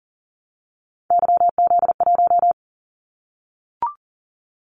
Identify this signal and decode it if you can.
Morse code, heard as a keyed tone, about 40 wpm, 700 Hz. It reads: YZ1
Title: Callsign Prefixes for Top 25 DXCC - 40wpm